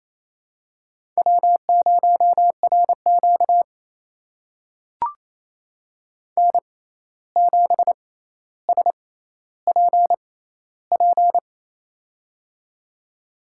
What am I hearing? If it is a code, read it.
Morse code, heard as a keyed tone, about 28 wpm, spaced as for 13 wpm, 700 Hz. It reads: W0RQ N7SPP
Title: USA Call Signs - 6x Character Spacing - 28wpm